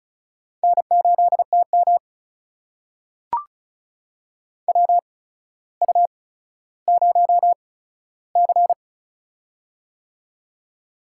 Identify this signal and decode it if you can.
Morse code, heard as a keyed tone, about 36 wpm, spaced as for 15 wpm, 700 Hz. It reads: N8TM WU0C
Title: USA Call Signs - 4 Characters - 8x Character Spacing - 35wpm